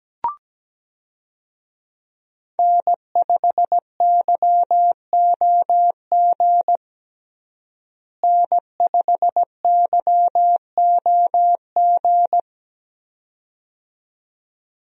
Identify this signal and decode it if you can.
Morse code, heard as a keyed tone, about 17 wpm, 700 Hz. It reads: N5YOG N5YOG
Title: USA Call Signs - 5 Characters - Repeated 2x - 17wpm